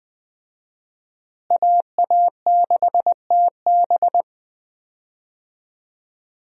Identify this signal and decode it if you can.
Morse code, heard as a keyed tone, about 20 wpm, 700 Hz. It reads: AA6TB
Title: USA Call Signs - 5 Characters - Repeated 2x - 20wpm